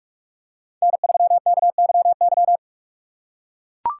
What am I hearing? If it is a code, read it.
Morse code, heard as a keyed tone, about 43 wpm, 700 Hz. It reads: N3KYY
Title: USA Call Signs - 5 Characters - 4x Character Spacing - 45wpm